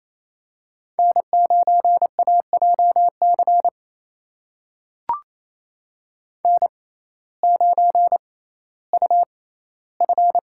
Morse code, heard as a keyed tone, about 29 wpm, spaced as for 16 wpm, 700 Hz. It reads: N9AJC N9UF
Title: USA Call Signs - 6x Character Spacing - 28wpm